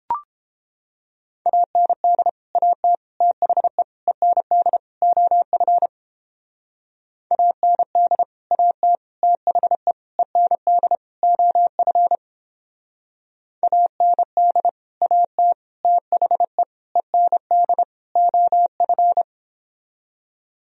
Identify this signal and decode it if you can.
Morse code, heard as a keyed tone, about 29 wpm, 700 Hz. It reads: AND AT THE END OF AND AT THE END OF AND AT THE END OF